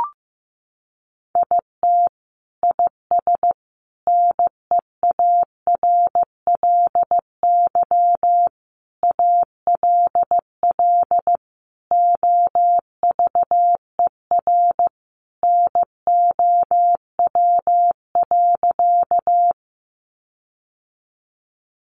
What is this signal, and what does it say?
Morse code, heard as a keyed tone, about 15 wpm, 700 Hz. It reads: IT IS NEARLY ALL OVER NOW.
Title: Sentences from Top 1300 Words 15wpm